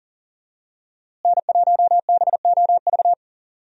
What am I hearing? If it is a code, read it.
Morse code, heard as a keyed tone, about 39 wpm, 700 Hz. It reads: N1BOV
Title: USA Call Signs - 5 Characters - Repeated 2x - 40wpm